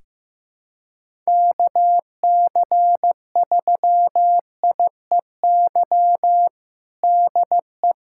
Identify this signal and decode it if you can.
Morse code, heard as a keyed tone, about 15 wpm, 700 Hz. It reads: KC3IEY DE
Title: USA Callsign DE Callsign 15wpm